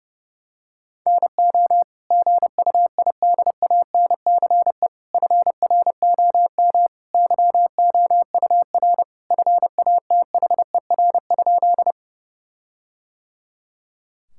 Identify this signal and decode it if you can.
Morse code, heard as a keyed tone, about 30 wpm, 700 Hz. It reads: NO GUIDANCE FROM YOUR FATHER?
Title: Sentences from Top 1600 Words 30wpm